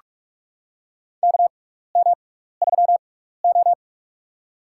Morse code, heard as a keyed tone, about 44 wpm, spaced as for 15 wpm, 700 Hz. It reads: KM3O
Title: USA Call Signs - 4 Characters - 6x Character Spacing - 45wpm